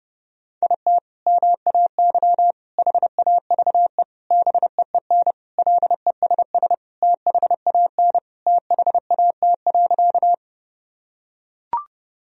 Morse code, heard as a keyed tone, about 30 wpm, 700 Hz. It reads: IT MAY HAVE BEEN LESS THAN THAT.
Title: Sentences from Top 1000 Words - Review Part 2 - 30wpm